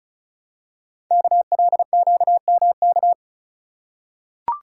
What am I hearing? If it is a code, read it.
Morse code, heard as a keyed tone, about 33 wpm, 700 Hz. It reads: KLQMK
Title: Code Groups - 5 Letters - 35wpm